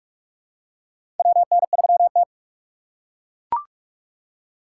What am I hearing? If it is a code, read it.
Morse code, heard as a keyed tone, about 43 wpm, 700 Hz. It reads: WN3T
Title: USA Call Signs - 4 Characters - 8x Character Spacing - 45wpm